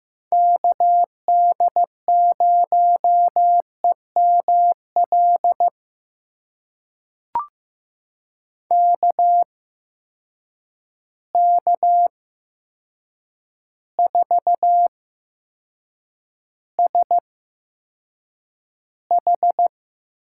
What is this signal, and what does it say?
Morse code, heard as a keyed tone, about 16 wpm, spaced as for 7 wpm, 700 Hz. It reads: KD0EML KK4SH
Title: USA Call Signs - 6 Characters - 8x Character Spacing - 15wpm